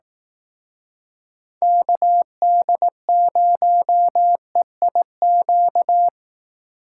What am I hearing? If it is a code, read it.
Morse code, heard as a keyed tone, about 18 wpm, 700 Hz. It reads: KD0EIQ
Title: Speed Racing - USA 2x3 Call Signs 15wpm